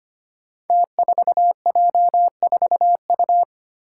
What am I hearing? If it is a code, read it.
Morse code, heard as a keyed tone, about 25 wpm, 700 Hz. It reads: T4J4U
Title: Five Challenging Letter-Number Combinations 25wpm